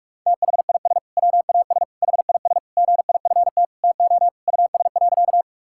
Morse code, heard as a keyed tone, about 41 wpm, 700 Hz. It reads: THIS WAS HIS GIFT TO US.
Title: Sentences from Top 1000 Words - Review Part 6 - 45wpm